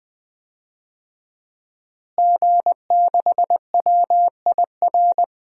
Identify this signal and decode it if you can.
Morse code, heard as a keyed tone, about 20 wpm, 700 Hz. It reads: G6WIR
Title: DXCC-100 Call Signs - 5 Characters - Repeated 3x - 20wpm